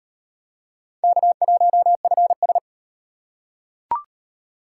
Morse code, heard as a keyed tone, about 38 wpm, 700 Hz. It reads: K1FS